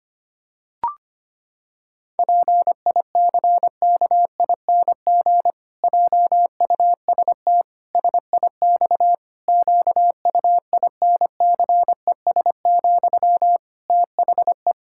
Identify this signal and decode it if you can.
Morse code, heard as a keyed tone, about 25 wpm, 700 Hz. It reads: PICKING JUST SIX QUINCES, THE
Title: Pangram Challenge v2 - 25wpm